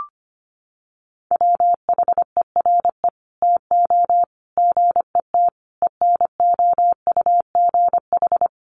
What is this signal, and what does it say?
Morse code, heard as a keyed tone, about 25 wpm, 700 Hz. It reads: WHERE TO GET ENOUGH
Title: Sentences from Top 1400 Words 25wpm